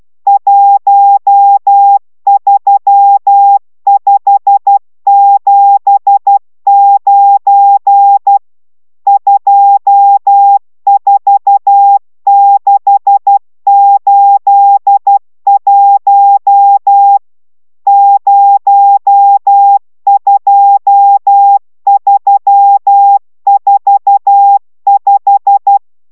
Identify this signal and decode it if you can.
Morse code, heard as a keyed tone, about 12 wpm, 800 Hz. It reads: 13579 24681 02345